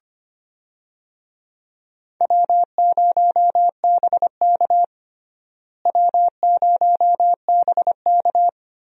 Morse code, heard as a keyed tone, about 25 wpm, 700 Hz. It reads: W0BK W0BK
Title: USA Call Signs - 4 Characters - Repeated 2x - 25wpm